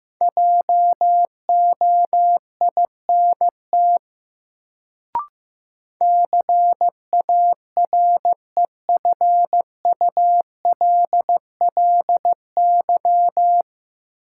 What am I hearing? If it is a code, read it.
Morse code, heard as a keyed tone, about 15 wpm, 700 Hz. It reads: JOINT CAREFULLY
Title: Sentences from Top 1300 Words - New Word Drill - 15wpm